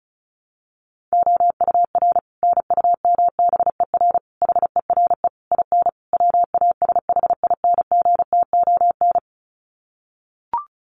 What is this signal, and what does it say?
Morse code, heard as a keyed tone, about 34 wpm, 700 Hz. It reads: OUR NUMBER HERE IN WASHINGTON
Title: Sets of 5 Words Encore2 35wpm